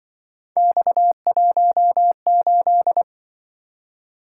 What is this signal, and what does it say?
Morse code, heard as a keyed tone, about 25 wpm, 700 Hz. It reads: X18
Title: Speed-Racing - Three Letter-Number Combinations - 20wpm